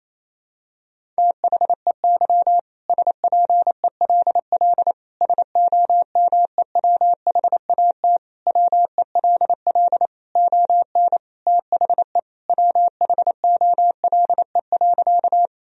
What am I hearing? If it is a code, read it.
Morse code, heard as a keyed tone, about 28 wpm, 700 Hz. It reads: THEY SPELL SOMEWHAT WELL ON THE WHOLE.
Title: Sentences from Top 1000 Words - Review Part 1 - 28wpm